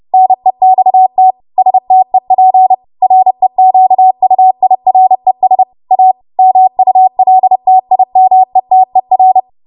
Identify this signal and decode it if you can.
Morse code, heard as a keyed tone, about 30 wpm, 750 Hz. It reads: NEXT STEP REQUIRES A MULTIMETER
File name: Archive / 30 WPM / 30WPM